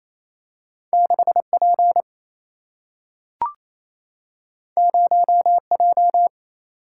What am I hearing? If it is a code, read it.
Morse code, heard as a keyed tone, about 28 wpm, 700 Hz. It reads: 6 P 0 J
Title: Single-Letter Single-Number 28wpm